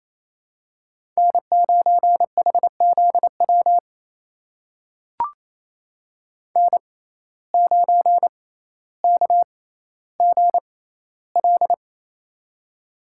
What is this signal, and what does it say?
Morse code, heard as a keyed tone, about 28 wpm, spaced as for 14 wpm, 700 Hz. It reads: N9HZW N9KGL
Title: USA Call Signs - 5 Characters - 6x Character Spacing - 28wpm